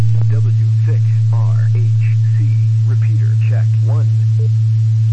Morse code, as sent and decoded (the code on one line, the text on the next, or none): none